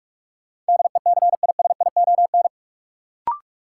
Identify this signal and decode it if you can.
Morse code, heard as a keyed tone, about 46 wpm, 700 Hz. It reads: DECISION
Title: Sentences from Top 1000 Words - New Word Drill - 45wpm